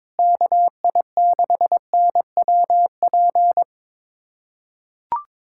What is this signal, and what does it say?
Morse code, heard as a keyed tone, about 22 wpm, 700 Hz. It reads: KI6NWP
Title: USA Callsign DE Callsign - 6 Characters - 22wpm